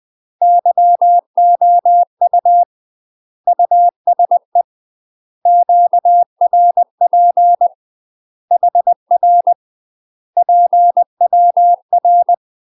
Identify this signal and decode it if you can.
Morse code, heard as a keyed tone, about 20 wpm, 700 Hz. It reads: YOU USE QRP HR PWR
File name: UK MASTER QSOs TRAINING / 20WPM / UK MASTER QSOs 0063